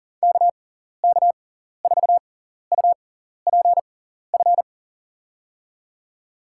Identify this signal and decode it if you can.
Morse code, heard as a keyed tone, about 40 wpm, spaced as for 14 wpm, 700 Hz. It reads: KK4UPF
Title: USA Call Signs - 6 Characters - 6x Character Spacing - 40wpm